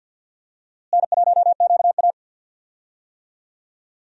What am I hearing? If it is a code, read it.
Morse code, heard as a keyed tone, about 48 wpm, 700 Hz. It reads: N1QA
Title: USA Call Signs - 4 Characters - Repeated 2x - 50wpm